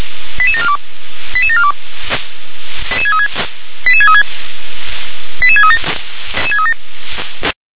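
Five-tone selective call (ZVEI), heard as quick runs of five tones, tones of about 70 ms. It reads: 88742 88742 88636 88637
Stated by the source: ZVEI five-tone selective call 88730, received by SDR.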